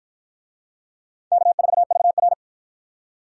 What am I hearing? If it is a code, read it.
Morse code, heard as a keyed tone, about 48 wpm, 700 Hz. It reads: K4VR